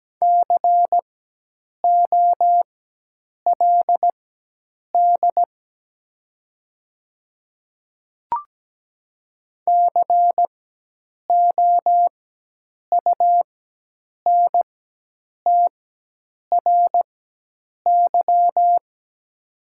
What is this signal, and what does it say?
Morse code, heard as a keyed tone, about 17 wpm, spaced as for 8 wpm, 700 Hz. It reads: COLD COUNTRY
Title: 4x Character Spacing - Top 500 Words - 17wpm